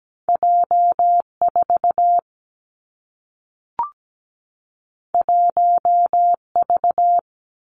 Morse code, heard as a keyed tone, about 17 wpm, 700 Hz. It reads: J4 1V
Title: Two Challenging Letter-Number Combinations 17wpm